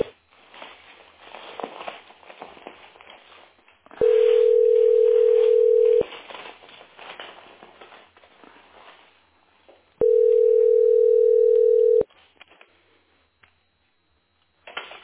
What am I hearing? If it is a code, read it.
Ringback tone (North America), 440 + 480 Hz.